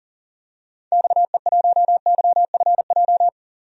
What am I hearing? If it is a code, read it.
Morse code, heard as a keyed tone, about 37 wpm, 700 Hz. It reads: XE1YFJ
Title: DXCC-100 Call Signs - 6 Characters - 40wpm